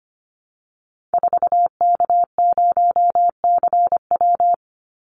Morse code, heard as a keyed tone, about 25 wpm, 700 Hz. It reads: 4K0CW
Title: Call Signs Repeated 3x - 25wpm